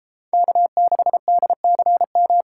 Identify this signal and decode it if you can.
Morse code, heard as a keyed tone, about 32 wpm, 700 Hz. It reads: K6DCM